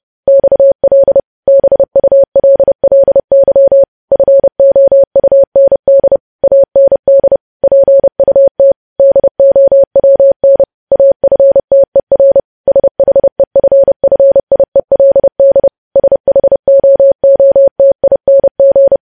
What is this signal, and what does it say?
Morse code, heard as a keyed tone, about 30 wpm, 550 Hz. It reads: XL BULLY FOUND AND PUT DOWN AFTER SHEFFIELD SHOOTING
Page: news headlines in morse code at 30 wpm